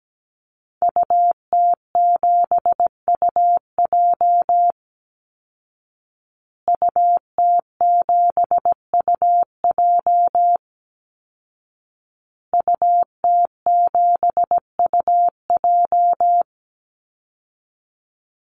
Morse code, heard as a keyed tone, about 17 wpm, 700 Hz. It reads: UT7UJ UT7UJ UT7UJ
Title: Call Signs Repeated 2x - 17wpm